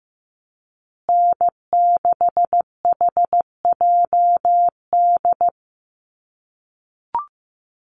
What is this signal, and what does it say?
Morse code, heard as a keyed tone, about 15 wpm, 700 Hz. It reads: N6HJD
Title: USA Call Signs Repeated 3x 15wpm